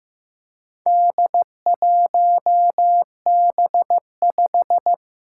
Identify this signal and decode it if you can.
Morse code, heard as a keyed tone, about 15 wpm, 700 Hz. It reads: D1B5
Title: Four Challenging Letter-Number Combinations 15wpm